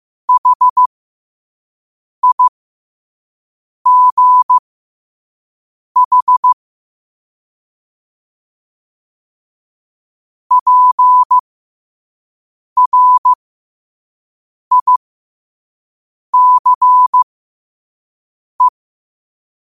Morse code, heard as a keyed tone, about 15 wpm, spaced as for 5 wpm, 1000 Hz. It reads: HIGH PRICE
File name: qotdfiles / 5wpm